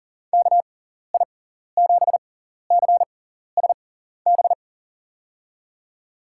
Morse code, heard as a keyed tone, about 40 wpm, spaced as for 14 wpm, 700 Hz. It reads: KI7CSB
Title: USA Call Signs - 6 Characters - 6x Character Spacing - 40wpm